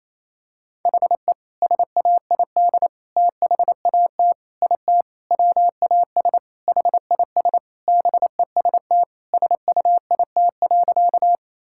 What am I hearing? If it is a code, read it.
Morse code, heard as a keyed tone, about 28 wpm, 700 Hz. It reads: HE SAID THAT IT WAS HIS BEST SUIT.
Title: Sentences from Top 1600 Words 28wpm